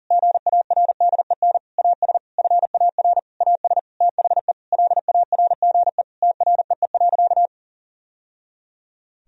Morse code, heard as a keyed tone, about 40 wpm, 700 Hz. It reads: GARDEN AS FAR AS THE LARGE TREE.
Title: Sentences from Top 1000 Words - Review Part 4 - 40wpm